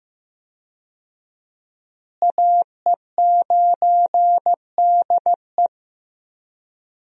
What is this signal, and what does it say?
Morse code, heard as a keyed tone, about 15 wpm, 700 Hz. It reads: AE9DE
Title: USA Call Signs - 5 Characters - Repeated 2x - 15wpm